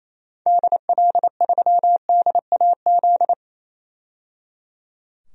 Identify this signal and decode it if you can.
Morse code, heard as a keyed tone, about 28 wpm, 700 Hz. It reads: DL3DAZ
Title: DXCC-100 Call Signs - 6 Characters - 28wpm